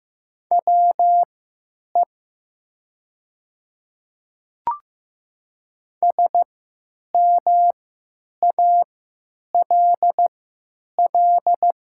Morse code, heard as a keyed tone, about 15 wpm, spaced as for 9 wpm, 700 Hz. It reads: WE SMALL
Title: 3x Character Spacing - Top 200 Words - 15wpm